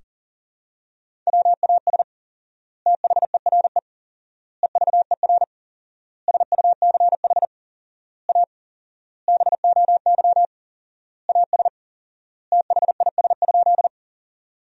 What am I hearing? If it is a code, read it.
Morse code, heard as a keyed tone, about 40 wpm, 700 Hz. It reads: WAS THERE EVER SUCH A BOY AS THIS?
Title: 4x Word-Spacing - Sentences from Top 400 Words - 40wpm